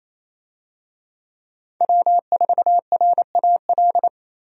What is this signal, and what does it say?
Morse code, heard as a keyed tone, about 28 wpm, 700 Hz. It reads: W4RAL
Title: USA Call Signs - 5 Characters - Repeated 2x - 28wpm